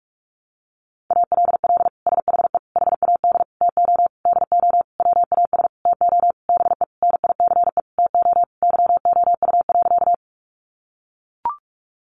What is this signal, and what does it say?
Morse code, heard as a keyed tone, about 44 wpm, 700 Hz. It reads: ALL SHE HAD TO DO WAS TO BE NICE TO YOU.
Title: Sentences from Top 1100 Words 45wpm